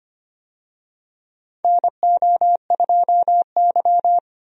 Morse code, heard as a keyed tone, about 25 wpm, 700 Hz. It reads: NO2Y
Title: USA Call Signs - 4 Characters - 25wpm